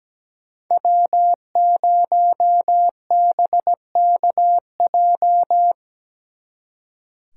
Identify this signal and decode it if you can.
Morse code, heard as a keyed tone, about 19 wpm, 700 Hz. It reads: W0BKJ
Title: Speed Racing - USA 1x3 Call Signs 17wpm